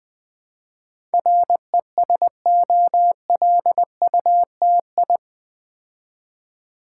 Morse code, heard as a keyed tone, about 20 wpm, 700 Hz. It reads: RESOLUTI
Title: Build A Word - 4 Letters At A Time - 10-Letter Words - 20wpm